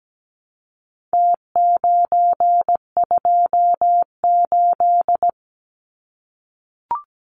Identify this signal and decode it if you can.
Morse code, heard as a keyed tone, about 17 wpm, 700 Hz. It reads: T928
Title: Numbers - Contest Style - 4-Digit 17wpm